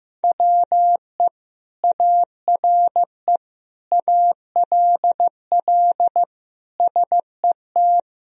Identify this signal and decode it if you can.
Morse code, heard as a keyed tone, about 15 wpm, 700 Hz. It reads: WE ARE ALL SET
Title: Sentences from Top 1000 Words - Review Part 1 - 15wpm